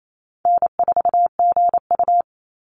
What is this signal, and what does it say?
Morse code, heard as a keyed tone, about 28 wpm, 700 Hz. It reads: N4GU
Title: Easy Call Signs - 28wpm